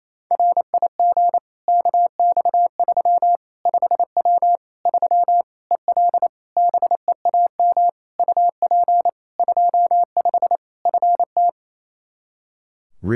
Morse code, heard as a keyed tone, about 28 wpm, 700 Hz. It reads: RIG KX3 5W 3 EL BEAM UP 25 FT
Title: K1USN SST QSO Practice 28wpm